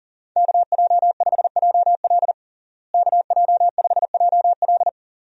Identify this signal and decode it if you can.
Morse code, heard as a keyed tone, about 37 wpm, 700 Hz. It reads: KJ5JL KJ5JL
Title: USA Call Signs - 5 Characters - Repeated 3x - 40wpm